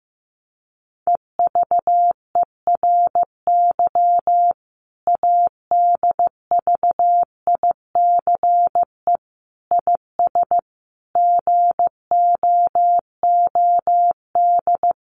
Morse code, heard as a keyed tone, about 15 wpm, 700 Hz. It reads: EVERY ADVICE IS GOOD
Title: Sentences from Top 1100 Words 15wpm